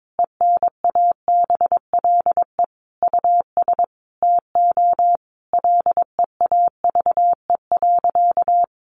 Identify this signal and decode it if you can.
Morse code, heard as a keyed tone, about 22 wpm, 700 Hz. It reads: ENABLE US TO LEAVE.
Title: Sentences from Top 1500 Words 22wpm